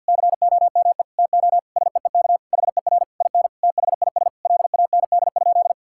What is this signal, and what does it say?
Morse code, heard as a keyed tone, about 50 wpm, 700 Hz. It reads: COME TO SEEK HER IN THIS LAND?
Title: Sentences from Top 1600 Words 50wpm